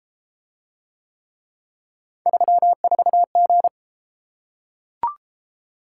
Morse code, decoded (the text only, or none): 34G